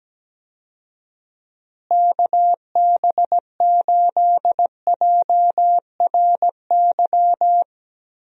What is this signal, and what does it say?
Morse code, heard as a keyed tone, about 17 wpm, 700 Hz. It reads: KB8JRY
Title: USA Call Signs - 6 Characters - Repeated 2x - 17wpm